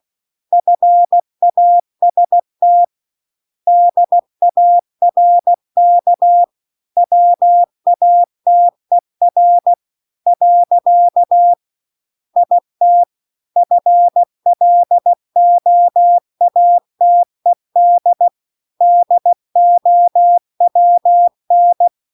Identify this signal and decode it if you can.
Morse code, heard as a keyed tone, about 16 wpm, 700 Hz. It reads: FAST DARK WATER . IT FLOATED DOWN
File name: CW-The-Big-Twoheaded-River-2-16WPM